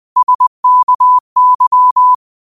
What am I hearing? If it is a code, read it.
Morse code, heard as a keyed tone, about 20 wpm, 1000 Hz. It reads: SKY
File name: qotdfiles / 20wpm